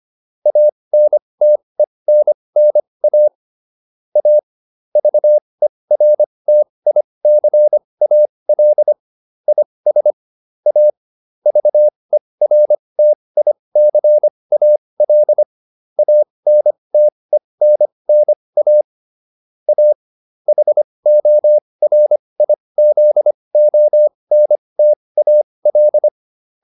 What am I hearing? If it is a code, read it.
Morse code, heard as a keyed tone, about 25 wpm, spaced as for 20 wpm, 600 Hz. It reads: ANTENNA A VERTICAL IS A VERTICAL ANTENNA A HORIZONTAL